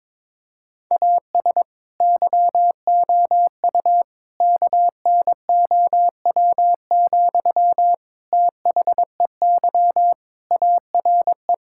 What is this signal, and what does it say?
Morse code, heard as a keyed tone, about 22 wpm, 700 Hz. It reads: AS YOU KNOW, THEY ARE
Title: Sentences from Top 1000 Words - Review Part 5 - 22wpm